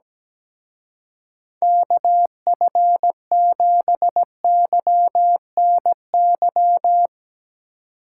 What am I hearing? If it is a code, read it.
Morse code, heard as a keyed tone, about 17 wpm, 700 Hz. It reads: KF7YNY
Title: Speed Racing - USA 2x3 Call Signs 17wpm